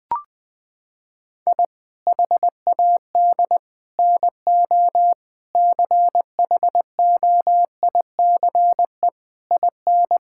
Morse code, heard as a keyed tone, about 20 wpm, 700 Hz. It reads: I HAD NO CHOICE IN
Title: Sentences from Top 1000 Words - Review Part 6 - 20wpm